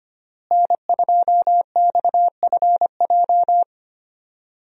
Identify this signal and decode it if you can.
Morse code, heard as a keyed tone, about 25 wpm, 700 Hz. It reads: N2XFJ
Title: USA Call Signs - 5 Characters - Repeated 3x - 25wpm